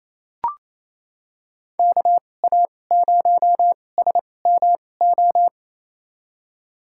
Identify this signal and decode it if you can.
Morse code, heard as a keyed tone, about 28 wpm, spaced as for 20 wpm, 700 Hz. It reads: KA0SMO
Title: USA Call Signs - 6 Characters - 2x Character Spacing - 28wpm